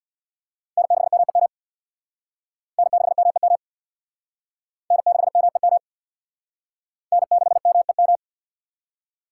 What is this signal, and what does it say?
Morse code, heard as a keyed tone, about 61 wpm, 700 Hz. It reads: N6MEM N6MEM N6MEM N6MEM
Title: Speed Racing - USA 1x3 Call Signs 50wpm